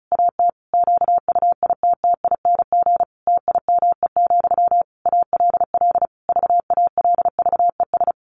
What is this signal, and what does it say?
Morse code, heard as a keyed tone, about 34 wpm, 700 Hz. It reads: AT QUITTING TIME, ALL VALVES
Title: A Sound Of Thunder 35wpm